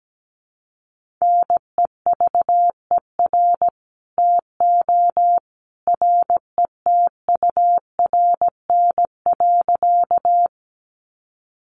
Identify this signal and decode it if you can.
Morse code, heard as a keyed tone, about 17 wpm, 700 Hz. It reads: NEVER TO RETURN.
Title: Sentences from Top 800 Words 17wpm